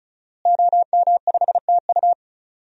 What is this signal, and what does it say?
Morse code, heard as a keyed tone, about 36 wpm, 700 Hz. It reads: OM5TU
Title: DXCC-100 Call Signs - 5 Characters - Repeated 3x - 35wpm